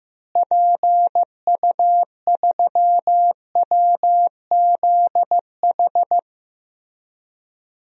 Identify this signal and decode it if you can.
Morse code, heard as a keyed tone, about 15 wpm, 700 Hz. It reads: PU3WZH
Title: DXCC-100 Call Signs - 6 Characters - Repeated 2x - 15wpm